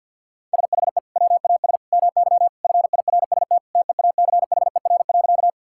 Morse code, heard as a keyed tone, about 50 wpm, 700 Hz. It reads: SHE WAS MY FIRST TEACHER.
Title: Sentences from Top 1000 Words - Review Part 6 - 50wpm